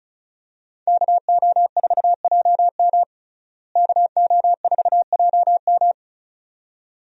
Morse code, heard as a keyed tone, about 36 wpm, 700 Hz. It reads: KO4JM KO4JM
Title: USA Call Signs - 5 Characters - Repeated 2x - 35wpm